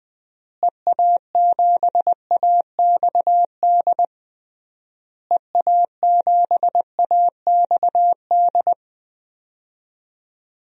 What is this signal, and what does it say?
Morse code, heard as a keyed tone, about 21 wpm, 700 Hz. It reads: EA7AXD EA7AXD
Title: DXCC-100 Call Signs - 6 Characters - Repeated 3x - 20wpm